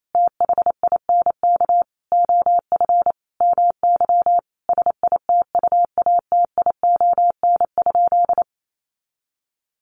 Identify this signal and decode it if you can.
Morse code, heard as a keyed tone, about 28 wpm, 700 Hz. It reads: THINK OF MY SITUATION?
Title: Sentences from Top 1200 Words 28wpm